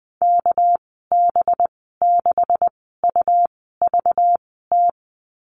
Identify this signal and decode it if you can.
Morse code, heard as a keyed tone, about 20 wpm, spaced as for 15 wpm, 700 Hz. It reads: KB6UVT
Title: 219 Lesson 35g - Number 0 - Call Signs 20wpm